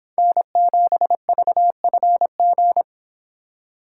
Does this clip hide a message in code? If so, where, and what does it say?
Yes: Morse code from 0.2 to 2.8 s, heard as a keyed tone, about 30 wpm, 700 Hz. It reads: N7VFG